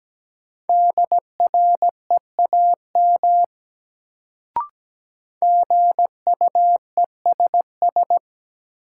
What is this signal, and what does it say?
Morse code, heard as a keyed tone, about 17 wpm, 700 Hz. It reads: DREAM GUESS
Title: Sentences from Top 1500 Words - New Word Drill - 17wpm